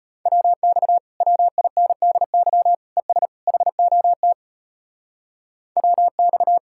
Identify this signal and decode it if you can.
Morse code, heard as a keyed tone, about 33 wpm, 700 Hz. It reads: WX WINDY ES HOT WX